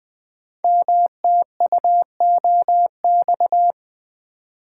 Morse code, heard as a keyed tone, about 20 wpm, 700 Hz. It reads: MTUOX
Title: Code Groups - 5 Letters - 20wpm